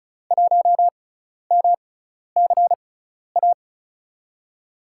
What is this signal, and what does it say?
Morse code, heard as a keyed tone, about 36 wpm, spaced as for 12 wpm, 700 Hz. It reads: 1MCA
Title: USA Call Signs - 6x Character Spacing - 35wpm